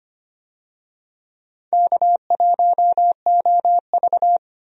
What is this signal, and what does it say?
Morse code, heard as a keyed tone, about 25 wpm, 700 Hz. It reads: K1OV
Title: USA Call Signs - 4 Characters - 25wpm